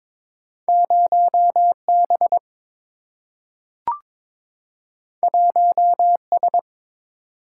Morse code, heard as a keyed tone, about 22 wpm, 700 Hz. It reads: 0B 1S
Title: Single-Letter Single-Number 22wpm